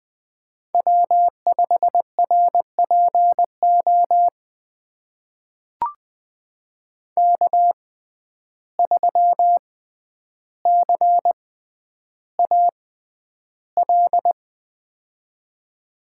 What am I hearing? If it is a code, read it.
Morse code, heard as a keyed tone, about 20 wpm, spaced as for 10 wpm, 700 Hz. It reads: W5RPO K3CAL
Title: USA Call Signs - 5 Characters - 6x Character Spacing - 20wpm